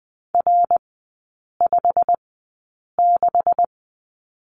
Morse code, heard as a keyed tone, about 20 wpm, spaced as for 8 wpm, 690 Hz. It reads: R56
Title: 212 Lesson 34e - Number 8 - Three Characters 20wpm